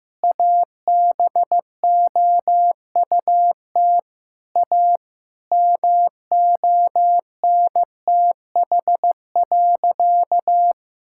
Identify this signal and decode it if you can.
Morse code, heard as a keyed tone, about 15 wpm, 700 Hz. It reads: ABOUT A MONTH.
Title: Sentences from Top 1000 Words - Review Part 6 - 15wpm